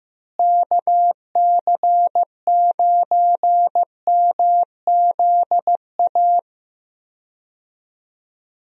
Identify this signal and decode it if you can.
Morse code, heard as a keyed tone, about 15 wpm, 700 Hz. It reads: KC9MZA
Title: USA Call Signs - 6 Characters - Repeated 3x - 15wpm